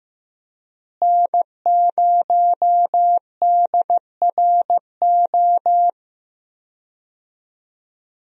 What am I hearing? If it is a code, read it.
Morse code, heard as a keyed tone, about 15 wpm, 700 Hz. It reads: N0DRO
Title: USA Call Signs - 5 Characters - Repeated 3x - 15wpm